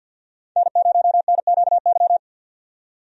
Morse code, heard as a keyed tone, about 52 wpm, 700 Hz. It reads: N0NQY
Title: USA Call Signs - 5 Characters - 50wpm